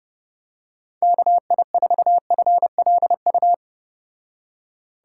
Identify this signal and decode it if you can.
Morse code, heard as a keyed tone, about 30 wpm, 700 Hz. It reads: KI4FLU